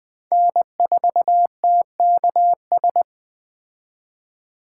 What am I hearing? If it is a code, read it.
Morse code, heard as a keyed tone, about 20 wpm, 700 Hz. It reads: N4TKS